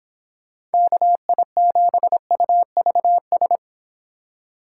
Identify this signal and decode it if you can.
Morse code, heard as a keyed tone, about 30 wpm, 700 Hz. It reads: KI7UVS